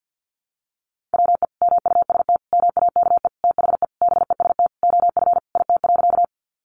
Morse code, heard as a keyed tone, about 50 wpm, 700 Hz. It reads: WE MUST MAKE THE BEST OF IT.
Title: Sentences from Top 400 Words 50wpm